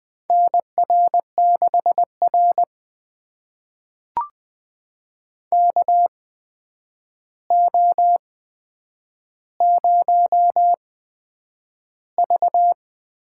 Morse code, heard as a keyed tone, about 20 wpm, spaced as for 9 wpm, 700 Hz. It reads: NR6R KO0V
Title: USA Call Signs - 4 Characters - 8x Character Spacing - 20wpm